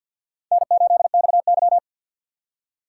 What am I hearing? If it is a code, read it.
Morse code, heard as a keyed tone, about 53 wpm, 700 Hz. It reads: N8XY
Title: USA Call Signs - 6x Character Spacing - 50wpm